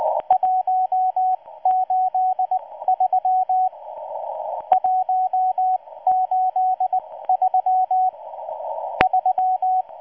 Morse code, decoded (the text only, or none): 183 183 3